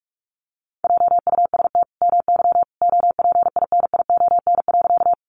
Morse code, heard as a keyed tone, about 44 wpm, 700 Hz. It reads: JUST MY OPINION.
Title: Sentences from Top 1200 Words 45wpm